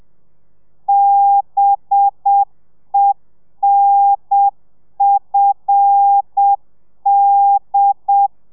Morse code, heard as a keyed tone, about 7 wpm, 800 Hz. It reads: BENFD
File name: telegrafia / i2tfj / Lez5 30